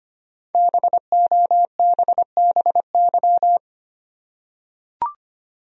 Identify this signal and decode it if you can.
Morse code, heard as a keyed tone, about 25 wpm, 700 Hz. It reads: BOBBY
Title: USA Common Names - 25wpm